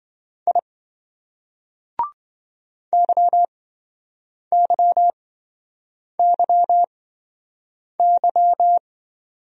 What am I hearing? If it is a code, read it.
Morse code, heard as a keyed tone, about 26 wpm, spaced as for 7 wpm, 700 Hz. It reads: I YYYY